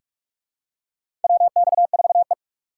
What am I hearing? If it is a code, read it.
Morse code, heard as a keyed tone, about 42 wpm, 700 Hz. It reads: WX4E
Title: USA Call Signs - 4 Characters - 3x Character Spacing - 45wpm